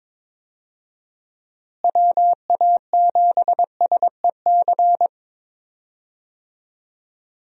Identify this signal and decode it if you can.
Morse code, heard as a keyed tone, about 22 wpm, 700 Hz. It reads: WA7SEC
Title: USA Call Signs - 6 Characters - Repeated 3x - 22wpm